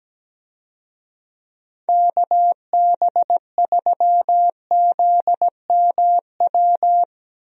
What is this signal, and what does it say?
Morse code, heard as a keyed tone, about 17 wpm, 700 Hz. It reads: KB3ZMW
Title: USA Call Signs - 6 Characters - 17wpm